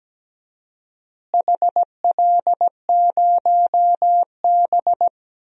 Morse code, heard as a keyed tone, about 17 wpm, 700 Hz. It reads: HL0B
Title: DXCC-100 Call Signs - 4 Characters - Repeated 2x - 17wpm